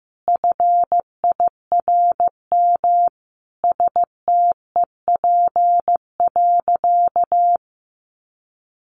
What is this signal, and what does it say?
Morse code, heard as a keyed tone, about 15 wpm, 700 Hz. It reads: FIRM STEP.
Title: Sentences from Top 1400 Words 15wpm